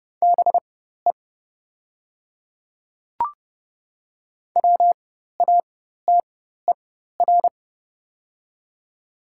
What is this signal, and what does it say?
Morse code, heard as a keyed tone, about 30 wpm, spaced as for 14 wpm, 700 Hz. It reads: BE WATER
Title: 4x Character Spacing - Top 200 Words - 30wpm